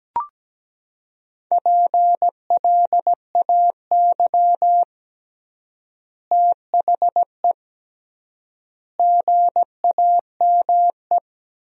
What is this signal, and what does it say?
Morse code, heard as a keyed tone, about 17 wpm, 700 Hz. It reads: PLAY THE GAME
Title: 3x Word-Spacing - Sentences from Top 500 Words - 17wpm